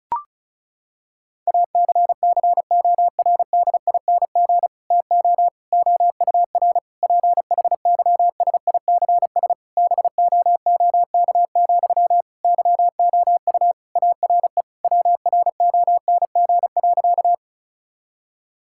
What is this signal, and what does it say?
Morse code, heard as a keyed tone, about 36 wpm, 700 Hz. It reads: ACCORDING TO OUR PHYSICS BOOK, YOU ARE WRONG.
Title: Sentences from Top 1000 Words - Review Part 5 - 35wpm